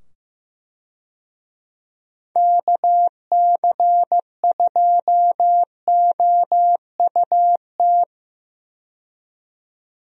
Morse code, heard as a keyed tone, about 15 wpm, 700 Hz. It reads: KC2OUT
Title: USA Call Signs - 6 Characters - 15wpm